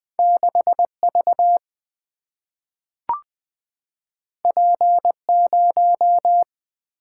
Morse code, heard as a keyed tone, about 20 wpm, 700 Hz. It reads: 6 V P 0
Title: Single-Letter Single-Number 20wpm